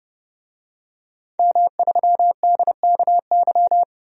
Morse code, heard as a keyed tone, about 30 wpm, 700 Hz. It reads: M3DKY
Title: DXCC-100 Call Signs - 5 Characters - 30wpm